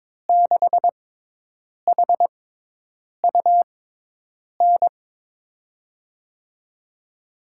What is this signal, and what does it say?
Morse code, heard as a keyed tone, about 22 wpm, spaced as for 8 wpm, 700 Hz. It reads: 6HUN
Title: USA Call Signs - 6 Characters - 6x Character Spacing - 22wpm